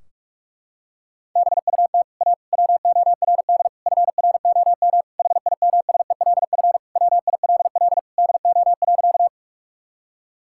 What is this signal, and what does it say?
Morse code, heard as a keyed tone, about 41 wpm, 700 Hz. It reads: BUT A WORD FROM HIMSELF WILL DO.
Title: Sentences from Top 1600 Words 45wpm